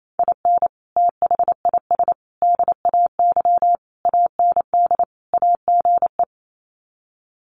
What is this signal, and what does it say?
Morse code, heard as a keyed tone, about 28 wpm, 700 Hz. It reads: IN THIS DAY AND AGE
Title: Sets of 5 Words Encore 28wpm